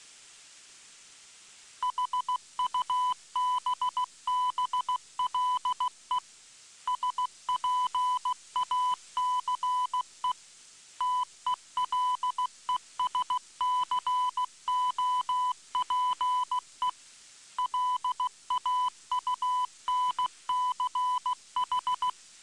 Morse code, heard as a keyed tone, about 16 wpm, 1000 Hz. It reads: HUBBLE SPACE TELESCOPE LAUNCH